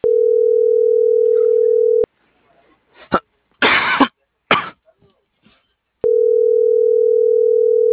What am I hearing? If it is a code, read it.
Ringback tone (North America), 440 + 480 Hz.
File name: CENTRAL MAERSK / out kevin.riosmsk